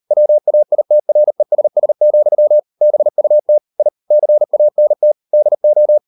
Morse code, heard as a keyed tone, about 38 wpm, 600 Hz. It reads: WAITRESS, BUT I CANT DO